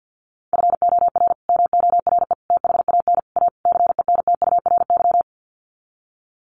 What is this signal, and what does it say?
Morse code, heard as a keyed tone, about 50 wpm, 700 Hz. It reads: FOR MORE THAN A CENTURY
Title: Sets of 5 Words Encore2 50wpm